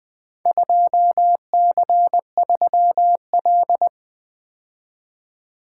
Morse code, heard as a keyed tone, about 20 wpm, 700 Hz. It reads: 2C3L
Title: Four Letter-Number Combinations - 20wpm